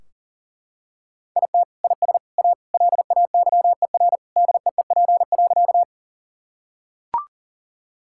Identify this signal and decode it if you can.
Morse code, heard as a keyed tone, about 40 wpm, 700 Hz. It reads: IT IS A LAYER DEEP.
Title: Sentences from Top 1600 Words 40wpm